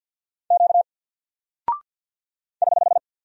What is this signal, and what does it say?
Morse code, heard as a keyed tone, about 48 wpm, 700 Hz. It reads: Q <HH>
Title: Single Character (Letter:Number:Punctuation:Prosign) Alt 50wpm